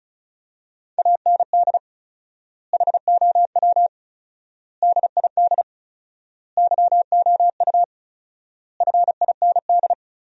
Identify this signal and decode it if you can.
Morse code, heard as a keyed tone, about 36 wpm, 700 Hz. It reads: AND HOW DID YOU FIND
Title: 4x Word-Spacing - Sentences from Top 100 Words - 35wpm